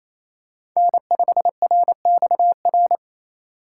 Morse code, heard as a keyed tone, about 28 wpm, 700 Hz. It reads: N5RXR
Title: USA Call Signs - 5 Characters - 2x Character Spacing - 28wpm